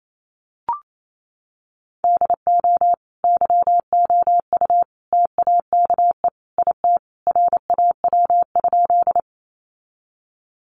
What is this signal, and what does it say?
Morse code, heard as a keyed tone, about 28 wpm, 700 Hz. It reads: DO YOU TAKE IT RAW?
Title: Sentences from Top 1500 Words 28wpm